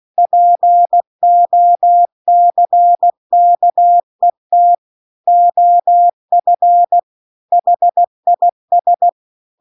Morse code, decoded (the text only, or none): POCKET OF HIS